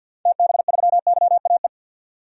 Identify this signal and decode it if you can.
Morse code, heard as a keyed tone, about 50 wpm, 700 Hz. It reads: TB3YAE
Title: DXCC-100 Call Signs - 6 Characters - 50wpm